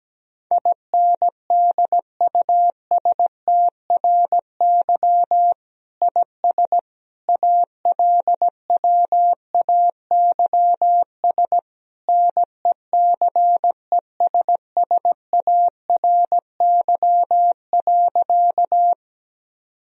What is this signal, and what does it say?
Morse code, heard as a keyed tone, about 17 wpm, 700 Hz. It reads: INDUSTRY IS ALWAYS NECESSARY.
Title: Sentences from Top 1000 Words - Review Part 4 - 17wpm